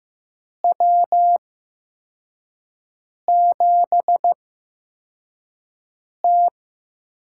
Morse code, heard as a keyed tone, about 15 wpm, spaced as for 4 wpm, 700 Hz. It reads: W7T
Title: USA Call Signs - 8x Character Spacing - 15wpm